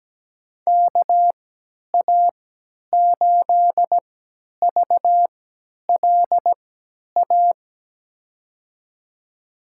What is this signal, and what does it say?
Morse code, heard as a keyed tone, about 17 wpm, spaced as for 10 wpm, 700 Hz. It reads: KA8VLA
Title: USA Call Signs - 6 Characters - 3x Character Spacing - 17wpm